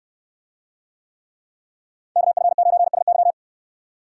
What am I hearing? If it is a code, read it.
Morse code, heard as a keyed tone, about 69 wpm, 690 Hz. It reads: KF9SY